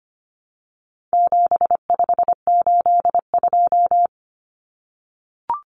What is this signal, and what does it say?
Morse code, heard as a keyed tone, about 25 wpm, 700 Hz. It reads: 7582
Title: Numbers - 4-Digit 25wpm